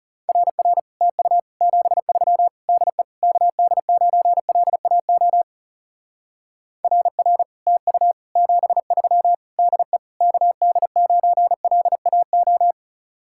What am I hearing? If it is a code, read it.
Morse code, heard as a keyed tone, about 36 wpm, 700 Hz. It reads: RR TU 73 DE KD9LAO RR TU 73 DE KD9LAO
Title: Speed-Racing Common QSO Phrases Encore - 30wpm